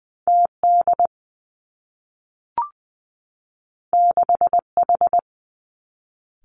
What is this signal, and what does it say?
Morse code, heard as a keyed tone, about 20 wpm, 700 Hz. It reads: T D 6 H
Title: Two Challenging Letter-Number Combinations 20wpm